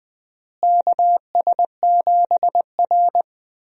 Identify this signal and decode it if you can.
Morse code, heard as a keyed tone, about 20 wpm, 700 Hz. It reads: KS7R
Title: USA Call Signs - 4 Characters - Repeated 3x - 20wpm